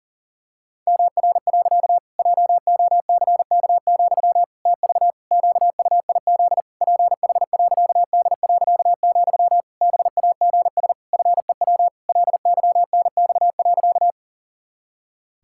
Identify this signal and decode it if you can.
Morse code, heard as a keyed tone, about 40 wpm, 700 Hz. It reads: MR. JOCK, TV QUIZ PH.D., BAGS FEW LYNX.
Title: Pangram Challenge v2 - 40wpm